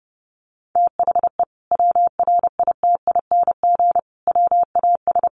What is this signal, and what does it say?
Morse code, heard as a keyed tone, about 30 wpm, 700 Hz. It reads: THE WRITING WAS
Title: Sentences from Top 1100 Words 30wpm